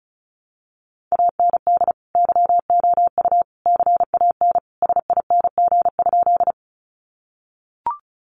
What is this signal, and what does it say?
Morse code, heard as a keyed tone, about 34 wpm, 700 Hz. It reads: AND YOU CAN SING?
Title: Sentences from Top 400 Words 35wpm